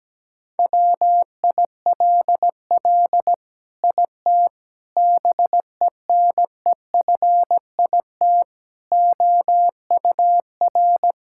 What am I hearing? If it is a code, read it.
Morse code, heard as a keyed tone, about 17 wpm, 700 Hz. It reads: WILL IT BENEFIT OUR
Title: Sentences from Top 1000 Words - Review Part 6 - 17wpm